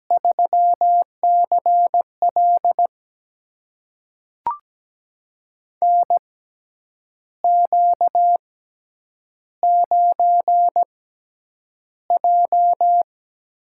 Morse code, heard as a keyed tone, about 17 wpm, spaced as for 8 wpm, 700 Hz. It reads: 3CL NQ9J
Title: USA Call Signs - 6x Character Spacing - 17wpm